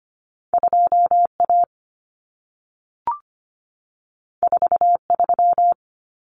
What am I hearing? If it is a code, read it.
Morse code, heard as a keyed tone, about 25 wpm, 700 Hz. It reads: 2A 43
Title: Common Cut Numbers - 2-Digit 25wpm